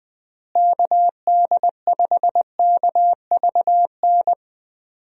Morse code, heard as a keyed tone, about 20 wpm, 700 Hz. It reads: KD5KVN
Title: USA Call Signs - 6 Characters - Repeated 3x - 20wpm